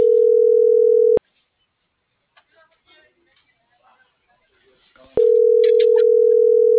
Ringback tone (North America), 440 + 480 Hz.